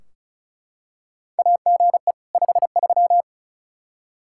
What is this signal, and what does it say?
Morse code, heard as a keyed tone, about 36 wpm, 700 Hz. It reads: AGE 53
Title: Practice QSO - Exchange 3 - More Info - 35wpm